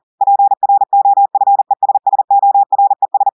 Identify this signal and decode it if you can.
Morse code, heard as a keyed tone, about 40 wpm, 800 Hz. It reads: PROFESSORES
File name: audios / portugais / portugais texte en clair 1 vitesse 40 mots minutes